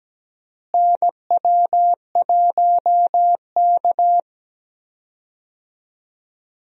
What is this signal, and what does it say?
Morse code, heard as a keyed tone, about 17 wpm, 700 Hz. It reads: NW1K
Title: USA Call Signs - 4 Characters - Repeated 3x - 17wpm